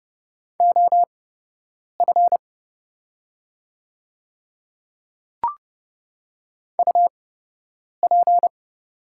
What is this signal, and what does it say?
Morse code, heard as a keyed tone, about 30 wpm, spaced as for 8 wpm, 700 Hz. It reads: OF UP
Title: 8x Character Spacing - Top 100 Words - 30wpm